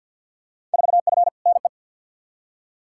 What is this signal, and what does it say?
Morse code, heard as a keyed tone, about 52 wpm, 700 Hz. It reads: 4F NE
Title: Field Day Exchanges Encore - 50wpm